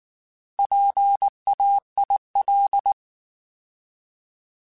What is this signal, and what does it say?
Morse code, heard as a keyed tone, about 19 wpm, 800 Hz. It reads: PAIL
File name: Alphabetical / Alphabetical mp3s / Ps @18